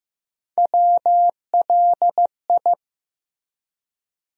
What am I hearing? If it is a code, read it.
Morse code, heard as a keyed tone, about 15 wpm, 700 Hz. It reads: WLI
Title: USA Call Sign Postfixes - 3 Letters 15wpm